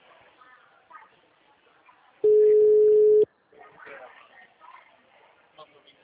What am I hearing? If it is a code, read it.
Ringback tone (Europe, 425 Hz), with speech.